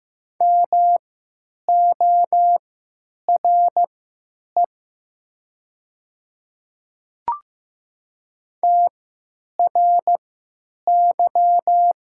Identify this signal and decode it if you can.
Morse code, heard as a keyed tone, about 15 wpm, spaced as for 9 wpm, 700 Hz. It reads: MORE TRY
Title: 3x Character Spacing - Top 200 Words - 15wpm